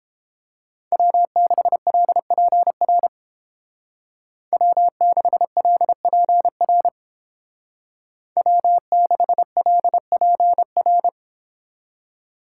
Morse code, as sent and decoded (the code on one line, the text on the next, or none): .-- -.... .-.. .--. .-. / .-- -.... .-.. .--. .-. / .-- -.... .-.. .--. .-.
W6LPR W6LPR W6LPR